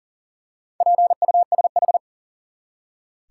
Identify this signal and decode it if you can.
Morse code, heard as a keyed tone, about 39 wpm, 700 Hz. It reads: PUSH
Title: Four Letter Words - 40wpm